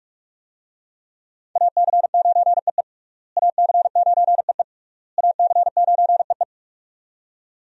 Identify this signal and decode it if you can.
Morse code, heard as a keyed tone, about 43 wpm, 700 Hz. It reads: AC9EE AC9EE AC9EE
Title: USA Call Signs - 5 Characters - Repeated 3x - 45wpm